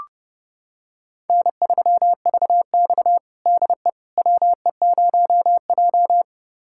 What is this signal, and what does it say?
Morse code, heard as a keyed tone, about 30 wpm, 700 Hz. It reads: N3VX DE WE0J
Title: USA Callsign DE Callsign - 4 Characters - 30wpm